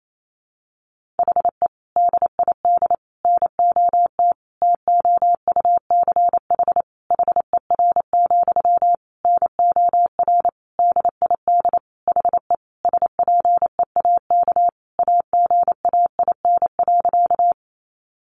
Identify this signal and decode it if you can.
Morse code, heard as a keyed tone, about 28 wpm, 700 Hz. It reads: HE DID NOT TOUCH HER, NOR DID HE SPEAK AGAIN.
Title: Sentences from Top 1500 Words 28wpm